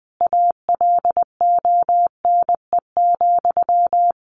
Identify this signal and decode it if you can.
Morse code, heard as a keyed tone, about 20 wpm, 700 Hz. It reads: ALONE,
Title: Sentences from Top 1000 Words 20wpm